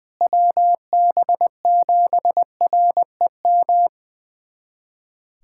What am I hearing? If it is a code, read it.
Morse code, heard as a keyed tone, about 20 wpm, 700 Hz. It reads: WB7REM
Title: USA Callsign DE Callsign - 6 Characters - 20wpm